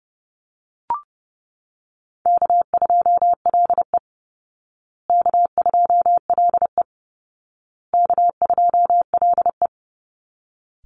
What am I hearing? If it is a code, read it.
Morse code, heard as a keyed tone, about 30 wpm, 700 Hz. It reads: K2LE K2LE K2LE
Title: Easy Call Signs Repeated 3x - 30wpm